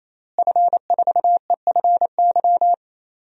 Morse code, heard as a keyed tone, about 28 wpm, 700 Hz. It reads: F4EFY
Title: DXCC-100 Call Signs - 5 Characters - Repeated 3x - 28wpm